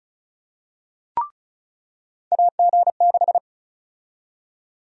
Morse code, heard as a keyed tone, about 34 wpm, 700 Hz. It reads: AG6
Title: USA Call Sign Prefixes - 35wpm